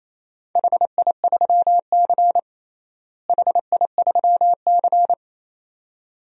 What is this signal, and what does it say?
Morse code, heard as a keyed tone, about 27 wpm, 700 Hz. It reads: HI3C HI3C
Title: DXCC-100 Call Signs - 4 Characters - Repeated 2x - 28wpm